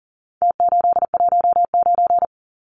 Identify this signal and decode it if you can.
Morse code, heard as a keyed tone, about 40 wpm, 700 Hz. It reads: T819
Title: Numbers - Contest Style - 4-Digit 40wpm